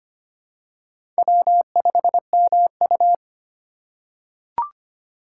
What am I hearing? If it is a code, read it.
Morse code, heard as a keyed tone, about 25 wpm, 700 Hz. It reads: W5MU
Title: USA Call Signs - 4 Characters - 6x Character Spacing - 25wpm